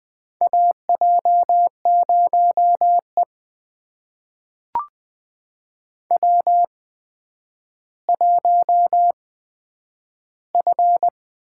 Morse code, heard as a keyed tone, about 20 wpm, spaced as for 10 wpm, 700 Hz. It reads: AJ0E W1F
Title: USA Call Signs - 4 Characters - 8x Character Spacing - 20wpm